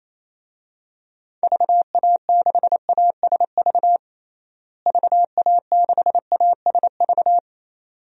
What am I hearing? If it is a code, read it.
Morse code, heard as a keyed tone, about 28 wpm, 700 Hz. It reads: VA6ASV VA6ASV
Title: DXCC-100 Call Signs - 6 Characters - Repeated 2x - 28wpm